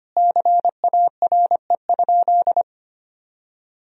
Morse code, heard as a keyed tone, about 25 wpm, 700 Hz. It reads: CARE?
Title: Sentences from Top 1000 Words - Review Part 1 - 25wpm